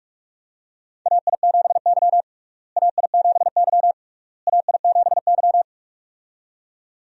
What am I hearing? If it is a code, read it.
Morse code, heard as a keyed tone, about 46 wpm, 700 Hz. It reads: AI7Y AI7Y AI7Y
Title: USA Call Signs - 4 Characters - Repeated 3x - 45wpm